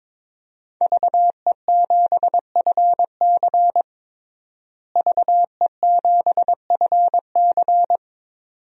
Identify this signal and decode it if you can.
Morse code, heard as a keyed tone, about 21 wpm, 700 Hz. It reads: VE7FC VE7FC
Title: DXCC-100 Call Signs - 5 Characters - Repeated 2x - 22wpm